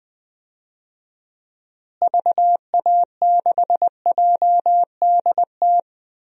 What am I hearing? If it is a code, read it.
Morse code, heard as a keyed tone, about 21 wpm, 700 Hz. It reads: VA6JDT
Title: DXCC-100 Call Signs - 6 Characters - Repeated 3x - 20wpm